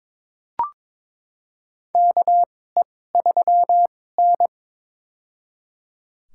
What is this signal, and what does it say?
Morse code, heard as a keyed tone, about 22 wpm, spaced as for 16 wpm, 700 Hz. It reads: KE3N
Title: USA Call Signs - 4 Characters - 2x Character Spacing - 22wpm